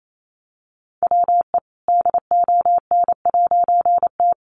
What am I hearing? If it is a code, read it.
Morse code, heard as a keyed tone, about 28 wpm, 700 Hz. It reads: WE DON'T
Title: A Sound Of Thunder 28wpm